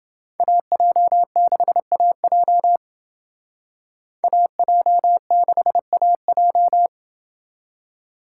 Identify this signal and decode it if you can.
Morse code, heard as a keyed tone, about 28 wpm, 700 Hz. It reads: AJ6AJ AJ6AJ